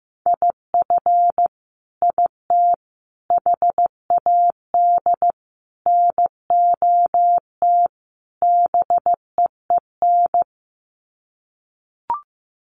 Morse code, heard as a keyed tone, about 15 wpm, 700 Hz. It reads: IF IT HAD NOT BEEN
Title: Sets of 5 Words Encore 15wpm